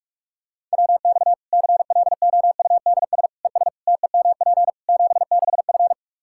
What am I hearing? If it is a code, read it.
Morse code, heard as a keyed tone, about 46 wpm, 700 Hz. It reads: WX CLOUDS ES TEMP 76F
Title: K1USN SST QSO Practice 45wpm